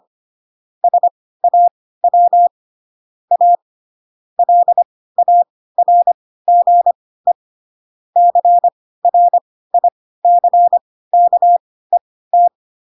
Morse code, heard as a keyed tone, about 25 wpm, spaced as for 16 wpm, 700 Hz. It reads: SAW A LARGE CRICKET
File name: Pinocchio / Pinocchio ch4 10 25 ik1hge